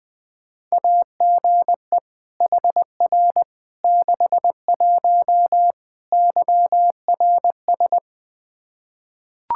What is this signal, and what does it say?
Morse code, heard as a keyed tone, about 20 wpm, 700 Hz. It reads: AGE HR 61 YRS
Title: Practice QSO - Exchange 3 - More Info - 20wpm